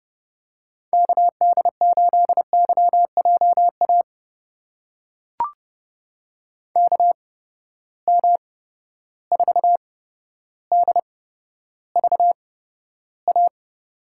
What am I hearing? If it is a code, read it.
Morse code, heard as a keyed tone, about 30 wpm, spaced as for 13 wpm, 700 Hz. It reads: KD8YJA KM4DVA
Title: USA Call Signs - 6 Characters - 8x Character Spacing - 30wpm